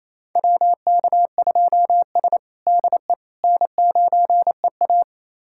Morse code, heard as a keyed tone, about 28 wpm, 700 Hz. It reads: WK2S DE N9EA
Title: USA Callsign DE Callsign - 4 Characters - 28wpm